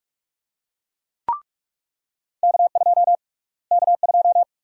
Morse code, heard as a keyed tone, about 44 wpm, 700 Hz. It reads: K2 K2
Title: USA Call Sign Prefixes - Repeated 2x - 45wpm